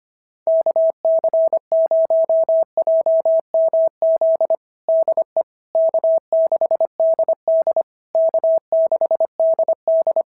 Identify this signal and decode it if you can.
Morse code, heard as a keyed tone, about 24 wpm, 650 Hz. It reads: KC0JMZ DE K6DD K6DD
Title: SOTA QSO Practice 25wpm